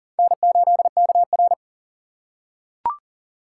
Morse code, heard as a keyed tone, about 40 wpm, 700 Hz. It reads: N8KR
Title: USA Call Signs - 4x Character Spacing - 40wpm